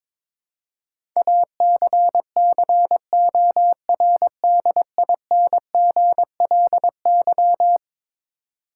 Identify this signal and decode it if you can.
Morse code, heard as a keyed tone, about 22 wpm, 700 Hz. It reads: ACCORDINGLY
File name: practice / 22wpm / Eleven Letter Words - 22wpm